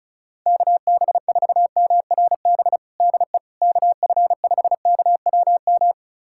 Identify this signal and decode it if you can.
Morse code, heard as a keyed tone, about 36 wpm, 700 Hz. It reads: KB4MRB DE KF5KWM
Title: USA Callsign DE Callsign 35wpm